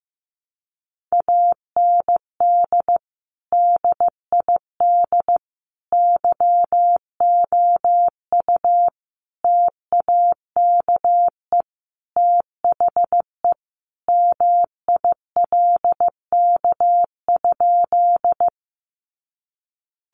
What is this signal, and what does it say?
Morse code, heard as a keyed tone, about 15 wpm, 700 Hz. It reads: AND DID YOU TAKE THE MILK?
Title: Sentences from Top 1500 Words 15wpm